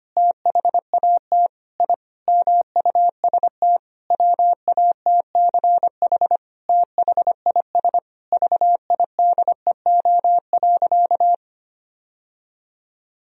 Morse code, heard as a keyed tone, about 25 wpm, 700 Hz. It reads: THAT I MUST WATCH THIS VIDEO.
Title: Sentences from Top 1000 Words - Review Part 4 - 25wpm